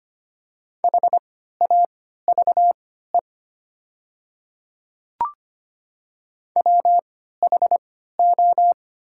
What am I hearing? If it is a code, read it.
Morse code, heard as a keyed tone, about 25 wpm, spaced as for 14 wpm, 690 Hz. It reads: HAVE WHO
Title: 3x Character Spacing - Top 100 Words - 25wpm